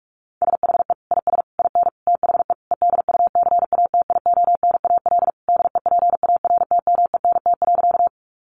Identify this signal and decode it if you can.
Morse code, heard as a keyed tone, about 44 wpm, 700 Hz. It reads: SHE IS IN THE EDUCATIONAL DEPARTMENT.
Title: Sentences from Top 1200 Words 45wpm